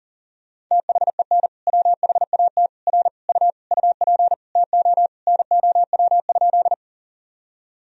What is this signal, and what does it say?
Morse code, heard as a keyed tone, about 40 wpm, 700 Hz. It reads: THEN WHAT R U UP TO NOW?
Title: Sentences from Top 100 Words and Abbreviations 40wpm